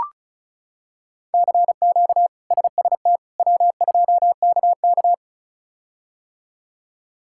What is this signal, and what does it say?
Morse code, heard as a keyed tone, about 36 wpm, 700 Hz. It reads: CQ SST W2KK
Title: K1USN SST QSO Practice - 35wpm